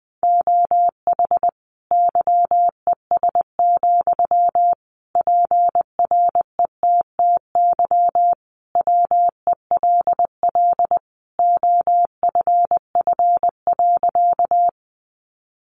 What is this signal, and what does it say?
Morse code, heard as a keyed tone, about 20 wpm, 700 Hz. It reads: OH YES, PRETTY WELL OFF.
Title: Sentences from Top 1000 Words 20wpm